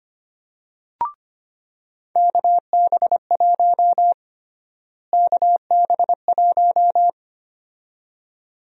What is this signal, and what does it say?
Morse code, heard as a keyed tone, about 25 wpm, 700 Hz. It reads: KB1 KB1
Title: USA Call Sign Prefixes - Repeated 2x - 25wpm